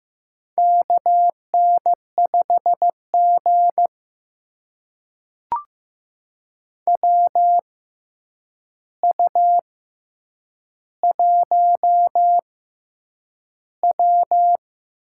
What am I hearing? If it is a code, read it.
Morse code, heard as a keyed tone, about 15 wpm, spaced as for 8 wpm, 700 Hz. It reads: KN5G WU1W
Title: USA Call Signs - 4 Characters - 6x Character Spacing - 15wpm